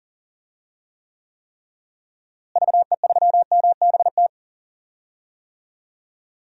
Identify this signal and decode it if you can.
Morse code, heard as a keyed tone, about 40 wpm, 700 Hz. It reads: VE3MBT
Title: DXCC-100 Call Signs - 6 Characters - Repeated 3x - 40wpm